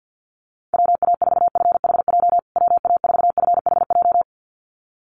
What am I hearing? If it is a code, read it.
Morse code, heard as a keyed tone, about 50 wpm, 700 Hz. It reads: WA4FHJ WA4FHJ
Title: USA Call Signs Repeated 2x 50wpm